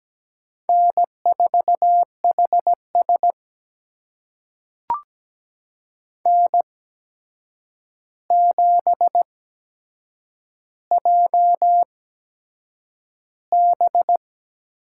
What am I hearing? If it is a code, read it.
Morse code, heard as a keyed tone, about 17 wpm, spaced as for 7 wpm, 700 Hz. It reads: N4HS N7JB
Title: USA Call Signs - 4 Characters - 8x Character Spacing - 17wpm